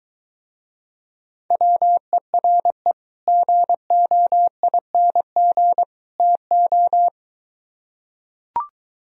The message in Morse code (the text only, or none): WERE GOING TO